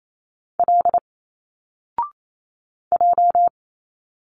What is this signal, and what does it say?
Morse code, heard as a keyed tone, about 28 wpm, 700 Hz. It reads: L J
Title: Single Letters 28wpm